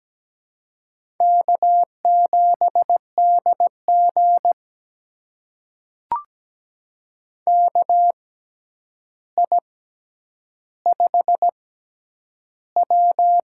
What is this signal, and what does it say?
Morse code, heard as a keyed tone, about 17 wpm, spaced as for 9 wpm, 700 Hz. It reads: K7DG KI5W
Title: USA Call Signs - 4 Characters - 6x Character Spacing - 17wpm